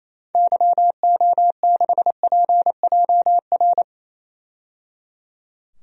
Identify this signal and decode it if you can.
Morse code, heard as a keyed tone, about 28 wpm, 700 Hz. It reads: YO6PJR
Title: DXCC-100 Call Signs - 6 Characters - 28wpm